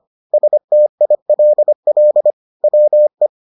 Morse code, heard as a keyed tone, about 25 wpm, 600 Hz. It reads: STILL WE